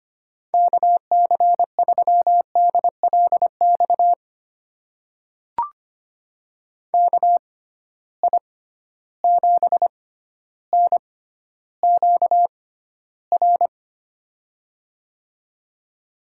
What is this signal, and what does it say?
Morse code, heard as a keyed tone, about 25 wpm, spaced as for 13 wpm, 700 Hz. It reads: KC3DLX KI7NQR
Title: USA Call Signs - 6 Characters - 6x Character Spacing - 25wpm